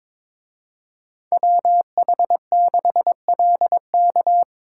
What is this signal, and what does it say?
Morse code, heard as a keyed tone, about 26 wpm, 700 Hz. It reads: WH6LK
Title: Speed Racing - USA 2x2 Call Signs 22wpm